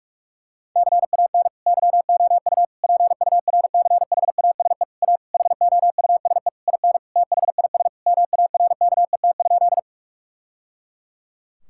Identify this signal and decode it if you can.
Morse code, heard as a keyed tone, about 46 wpm, 700 Hz. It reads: CAN YOU PURCHASE A HOUSE IN THIS MARKET?
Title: Sentences from Top 1000 Words - Review Part 5 - 45wpm